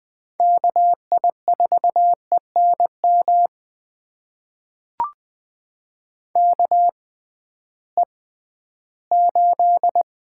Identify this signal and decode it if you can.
Morse code, heard as a keyed tone, about 20 wpm, spaced as for 13 wpm, 700 Hz. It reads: KI4ENM KE8
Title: USA Call Signs - 6 Characters - 6x Character Spacing - 20wpm